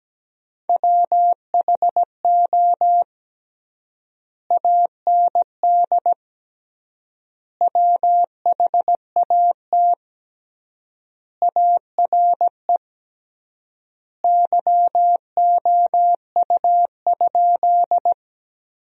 Morse code, heard as a keyed tone, about 17 wpm, 700 Hz. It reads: WHO AND WHAT ARE YOU?
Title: 3x Word-Spacing - Sentences from Top 100 Words - 17wpm